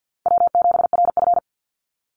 Morse code, heard as a keyed tone, about 50 wpm, 700 Hz. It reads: W7RF
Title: Easy Call Signs Repeated 3x - 50wpm